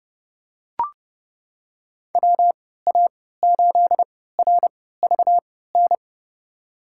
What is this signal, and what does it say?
Morse code, heard as a keyed tone, about 30 wpm, spaced as for 17 wpm, 700 Hz. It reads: WA8RVN
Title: USA Call Signs - 6 Characters - 3x Character Spacing - 30wpm